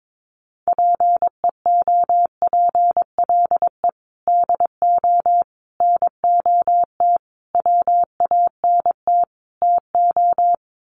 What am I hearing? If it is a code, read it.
Morse code, heard as a keyed tone, about 22 wpm, 700 Hz. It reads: PEOPLE DO NOT WANT TO
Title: Sets of 5 Words Encore2 22wpm